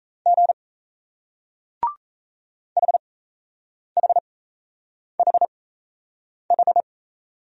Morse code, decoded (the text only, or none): G HHHH